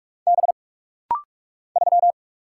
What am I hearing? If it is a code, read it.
Morse code, heard as a keyed tone, about 44 wpm, 700 Hz. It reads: B 3
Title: Single Letter-Number - Rapid-Fire 45wpm